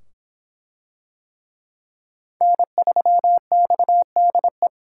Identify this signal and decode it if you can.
Morse code, heard as a keyed tone, about 25 wpm, 700 Hz. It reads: N3XDE